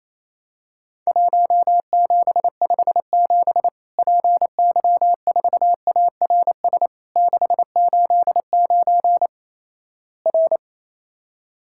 Morse code, heard as a keyed tone, about 28 wpm, 700 Hz. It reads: 1757 PY4ARS 689 R
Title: Worked All Europe DX Contest - QTC Exchange - 28wpm